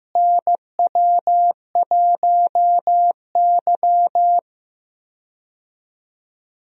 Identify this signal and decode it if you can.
Morse code, heard as a keyed tone, about 15 wpm, 700 Hz. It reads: NW1Y
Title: USA Call Signs - 4 Characters - Repeated 2x - 15wpm